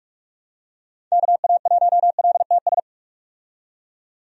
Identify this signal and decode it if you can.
Morse code, heard as a keyed tone, about 44 wpm, 700 Hz. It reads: KA1LTS
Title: USA Call Signs - 6 Characters - 45wpm